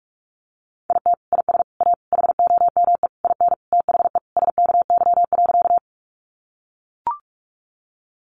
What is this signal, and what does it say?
Morse code, heard as a keyed tone, about 44 wpm, 700 Hz. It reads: IT IS A HOME IN THE SKY.
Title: Sentences from Top 1200 Words 45wpm